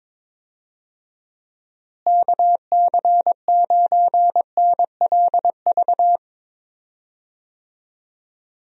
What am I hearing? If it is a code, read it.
Morse code, heard as a keyed tone, about 22 wpm, 700 Hz. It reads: KC9NLV
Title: USA Call Signs - 6 Characters - Repeated 3x - 22wpm